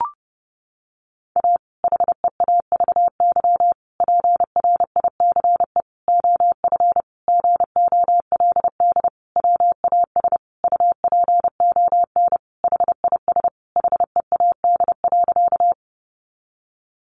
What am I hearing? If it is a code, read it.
Morse code, heard as a keyed tone, about 30 wpm, 700 Hz. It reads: A HEAVY PRICE OF GOLD WAS UPON HIS HEAD.
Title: Sentences from Top 1100 Words 30wpm